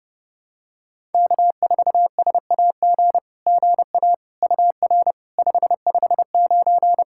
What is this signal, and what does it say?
Morse code, heard as a keyed tone, about 30 wpm, 690 Hz. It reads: K4SAG GA UR 559
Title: SOTA QSO Practice 30wpm